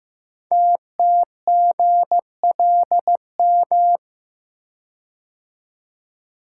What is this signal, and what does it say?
Morse code, heard as a keyed tone, about 15 wpm, 700 Hz. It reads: TTGLM
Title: Code Groups - 5 Letters - 15wpm